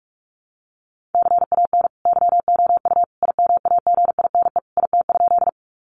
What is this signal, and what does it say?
Morse code, heard as a keyed tone, about 44 wpm, 700 Hz. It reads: CAN YOU IMAGINE IT?
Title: Sentences from Top 1400 Words 45wpm